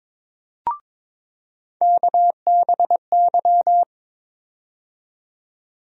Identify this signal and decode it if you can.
Morse code, heard as a keyed tone, about 21 wpm, 700 Hz. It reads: KBY
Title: USA Call Sign Postfixes - 3 Letters 22wpm